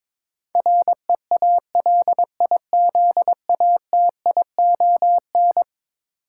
Morse code, heard as a keyed tone, about 22 wpm, 700 Hz. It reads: REALIZATION
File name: practice / 22wpm / Eleven Letter Words - 22wpm